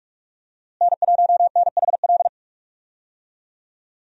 Morse code, heard as a keyed tone, about 46 wpm, 700 Hz. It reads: N1NHL
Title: USA Call Signs - 5 Characters - Repeated 2x - 45wpm